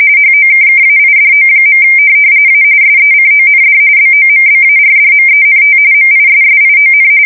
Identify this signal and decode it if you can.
RTTY, about 45.45 baud. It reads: HOKKAIDO. A SECOND BOMBARDMENT MISSION
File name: fsk170 / USS-Missouri